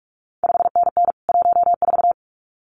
Morse code, heard as a keyed tone, about 46 wpm, 700 Hz. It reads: 5NN 14
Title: CQ WW DX Contest Exchange Encore - 45wpm